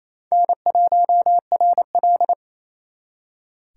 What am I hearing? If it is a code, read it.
Morse code, heard as a keyed tone, about 37 wpm, 700 Hz. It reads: N1RL